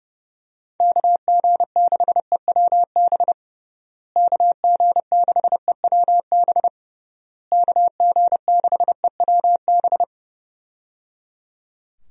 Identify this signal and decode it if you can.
Morse code, heard as a keyed tone, about 29 wpm, 700 Hz. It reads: KG6EWB KG6EWB KG6EWB
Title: USA Call Signs - 6 Characters - Repeated 3x - 30wpm